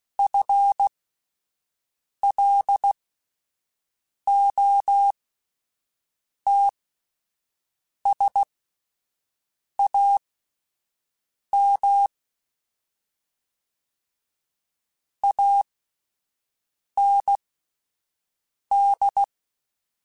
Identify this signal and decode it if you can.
Morse code, heard as a keyed tone, about 16 wpm, spaced as for 5 wpm, 800 Hz. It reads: FLOTSAM AND
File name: Misc / Misc-mp3s / WordPairs sp @10